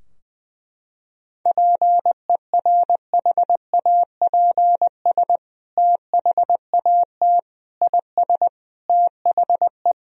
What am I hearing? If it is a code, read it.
Morse code, heard as a keyed tone, about 21 wpm, 700 Hz. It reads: PERHAPS THAT IS THE
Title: Sentences from Top 1000 Words - Review Part 4 - 20wpm